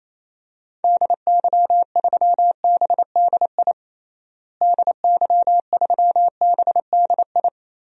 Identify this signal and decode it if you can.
Morse code, heard as a keyed tone, about 28 wpm, 700 Hz. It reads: DY3BDI DY3BDI
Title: DXCC-100 Call Signs - 6 Characters - Repeated 3x - 28wpm